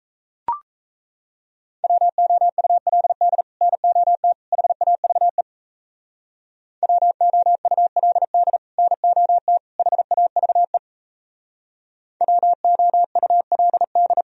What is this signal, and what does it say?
Morse code, heard as a keyed tone, about 36 wpm, 700 Hz. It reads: WOULD NOT HAVE WOULD NOT HAVE WOULD